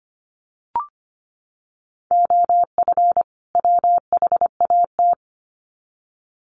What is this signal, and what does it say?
Morse code, heard as a keyed tone, about 25 wpm, 700 Hz. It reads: OF WHAT
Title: Sets of 2 Words 25wpm